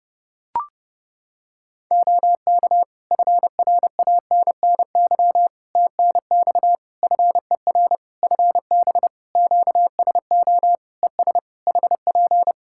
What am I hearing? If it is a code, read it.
Morse code, heard as a keyed tone, about 30 wpm, 700 Hz. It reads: OK FRANNY TNX FER FB QSO ES HP
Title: Practice QSO - Exchange 4 - Ending - 30wpm